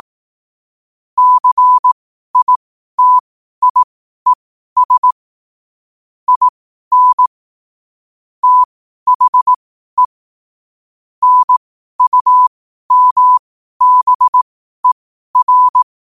Morse code, heard as a keyed tone, about 18 wpm, spaced as for 12 wpm, 1000 Hz. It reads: CITIES IN THE NUMBER
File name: qotdfiles / 12wpm